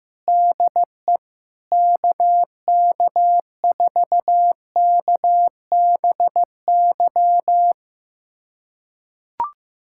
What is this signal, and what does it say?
Morse code, heard as a keyed tone, about 15 wpm, 700 Hz. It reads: DE KK4KBY
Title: USA Callsign DE Callsign - 6 Characters - 15wpm